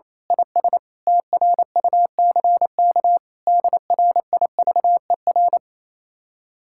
Morse code, heard as a keyed tone, about 26 wpm, 700 Hz. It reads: IS TRUCK DRIVER
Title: Speed-Racing Common QSO Phrases - 28wpm